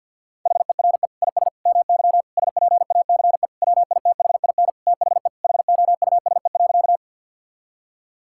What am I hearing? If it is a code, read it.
Morse code, heard as a keyed tone, about 49 wpm, 700 Hz. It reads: HERE IS MY SPACE WITHIN THE HOUSE.
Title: Sentences from Top 1000 Words - Review Part 5 - 50wpm